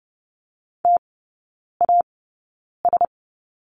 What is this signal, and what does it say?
Morse code, heard as a keyed tone, about 30 wpm, spaced as for 9 wpm, 690 Hz. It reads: TAS
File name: Lesson 16e - Letter c - Three Characters 30wpm